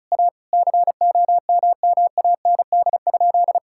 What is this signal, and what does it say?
Morse code, heard as a keyed tone, about 34 wpm, 700 Hz. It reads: A COMMAND?
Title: Sentences from Top 1600 Words 35wpm